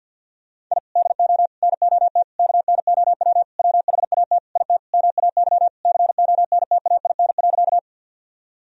Morse code, heard as a keyed tone, about 49 wpm, 700 Hz. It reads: I DO NOT KNOW WHAT IT MAY CONTAIN.
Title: Sentences from Top 1000 Words - Review Part 2 - 50wpm